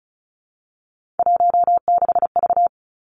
Morse code, heard as a keyed tone, about 36 wpm, 700 Hz. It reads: 16V
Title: Three Challenging Letter-Number Combinations 35wpm